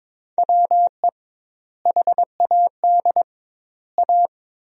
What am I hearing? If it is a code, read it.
Morse code, heard as a keyed tone, about 22 wpm, 700 Hz. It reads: WE HAD A
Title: 2x Word-Spacing - Sentences from Top 400 Words - 22wpm